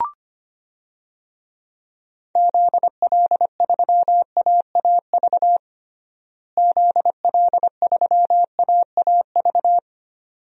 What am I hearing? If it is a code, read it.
Morse code, heard as a keyed tone, about 25 wpm, 700 Hz. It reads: ZL3AAV ZL3AAV
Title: DXCC-100 Call Signs - 6 Characters - Repeated 2x - 25wpm